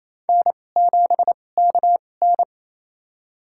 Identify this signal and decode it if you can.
Morse code, heard as a keyed tone, about 27 wpm, spaced as for 20 wpm, 700 Hz. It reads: N7KN
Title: USA Call Signs - 4 Characters - 2x Character Spacing - 28wpm